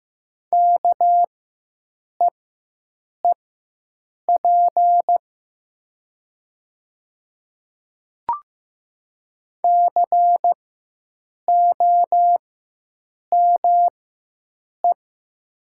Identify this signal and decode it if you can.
Morse code, heard as a keyed tone, about 15 wpm, spaced as for 7 wpm, 700 Hz. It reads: KEEP COME
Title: 4x Character Spacing - Top 300 Words - 15wpm